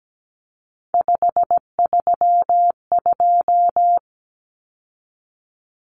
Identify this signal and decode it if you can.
Morse code, heard as a keyed tone, about 17 wpm, 700 Hz. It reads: 532
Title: Numbers - 3-Digit 17wpm